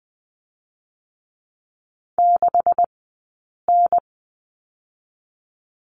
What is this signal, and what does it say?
Morse code, heard as a keyed tone, about 20 wpm, spaced as for 8 wpm, 700 Hz. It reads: 6N
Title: 126 Lesson 21d - Number 6 - Two Characters 20wpm